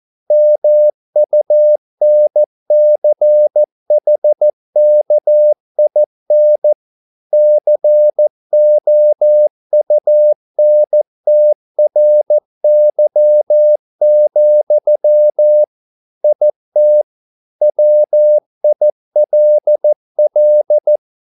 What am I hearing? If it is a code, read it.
Morse code, heard as a keyed tone, about 14 wpm, 600 Hz. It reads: MUNCHKIN COUNTRY, IT WILL